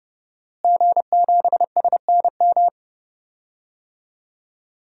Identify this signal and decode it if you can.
Morse code, heard as a keyed tone, about 30 wpm, 700 Hz. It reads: G7SNM
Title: DXCC-100 Call Signs - 5 Characters - Repeated 2x - 30wpm